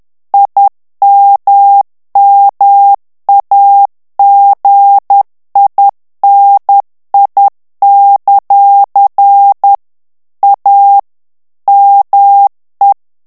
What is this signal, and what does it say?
Morse code, heard as a keyed tone, about 10 wpm, 800 Hz. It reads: IMMAGINI; A ME
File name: morse mp3 / ITALIEN texte clair 1 vitesse 10 mots minute